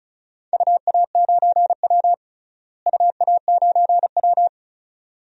Morse code, heard as a keyed tone, about 34 wpm, 700 Hz. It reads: UA9W UA9W
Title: DXCC-100 Call Signs - 4 Characters - Repeated 2x - 35wpm